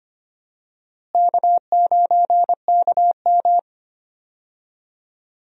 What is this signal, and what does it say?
Morse code, heard as a keyed tone, about 25 wpm, 700 Hz. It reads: K9KM
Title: USA Call Signs - 4 Characters - Repeated 3x - 25wpm